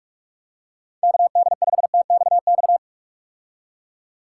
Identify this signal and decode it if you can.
Morse code, heard as a keyed tone, about 47 wpm, 700 Hz. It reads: KD5TXX
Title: K1USN SST QSO Practice - 45wpm